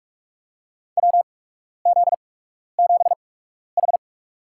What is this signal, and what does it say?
Morse code, heard as a keyed tone, about 46 wpm, spaced as for 12 wpm, 700 Hz. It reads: WZ7H
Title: USA Call Signs - 4 Characters - 8x Character Spacing - 45wpm